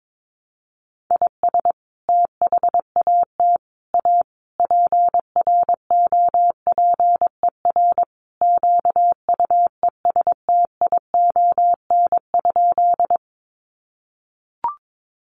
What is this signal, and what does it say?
Morse code, heard as a keyed tone, about 22 wpm, 700 Hz. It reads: IS THAT A PROPER QUESTION?
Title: Sentences from Top 1100 Words 22wpm